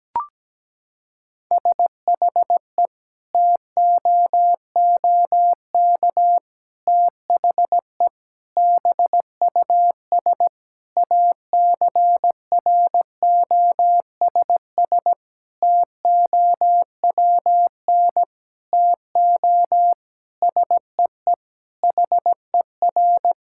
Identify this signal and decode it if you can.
Morse code, heard as a keyed tone, about 17 wpm, 700 Hz. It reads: SHE TOOK THE BUS ACROSS TOWN TO SEE HER
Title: Sentences from Top 1000 Words - Review Part 6 - 17wpm